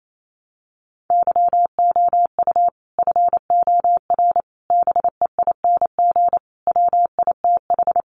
Morse code, heard as a keyed tone, about 28 wpm, 700 Hz. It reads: YOU FOR BEING WITH
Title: Sets of 5 Words 28wpm